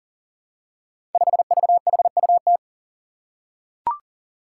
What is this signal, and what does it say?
Morse code, heard as a keyed tone, about 42 wpm, 700 Hz. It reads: 5VHUT
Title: Five Challenging Letter-Number Combinations 40wpm